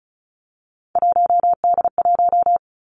Morse code, heard as a keyed tone, about 35 wpm, 700 Hz. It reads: 1D1
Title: Three Challenging Letter-Number Combinations 35wpm